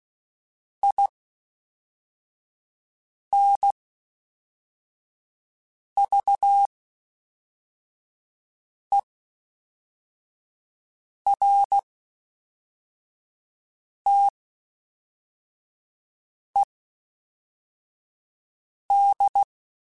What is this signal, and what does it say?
Morse code, heard as a keyed tone, about 16 wpm, spaced as for 4 wpm, 800 Hz. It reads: INVERTED
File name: (spaced) / FromSMs spaced mp3s / Classes@5